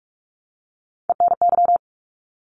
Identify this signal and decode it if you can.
Morse code, heard as a keyed tone, about 46 wpm, 700 Hz. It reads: ENY
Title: ARRL Sections 45wpm